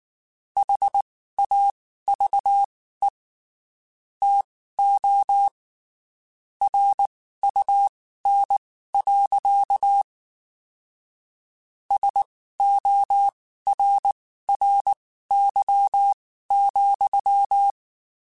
Morse code, heard as a keyed tone, about 19 wpm, spaced as for 14 wpm, 800 Hz. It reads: HAVE TO RUN. SORRY,
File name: (spaced) / QSOs Spaced-mp3s / kc7haa @18 sp